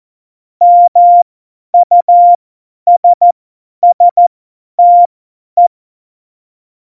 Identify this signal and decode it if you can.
Morse code, heard as a keyed tone, about 14 wpm, spaced as for 10 wpm, 700 Hz. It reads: MUSSTE
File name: Morsekurs / Witze / Witz 11. Februar 2020, 50 F70